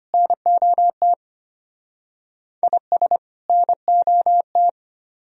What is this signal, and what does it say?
Morse code, heard as a keyed tone, about 31 wpm, 700 Hz. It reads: NOT IS NOT
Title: Speed-Racing - Sets of 2 Words 25wpm